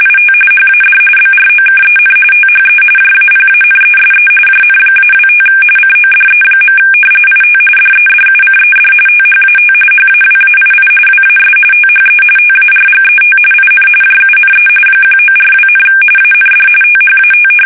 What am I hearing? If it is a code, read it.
RTTY, about 45.45 baud. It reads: IT HAD BROKEN THE JAPANESE JN-25 CODE. IT DEDUCED THAT HAWAII WAS THE TARGET IN JUNE 1942, AND THAT
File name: fsk850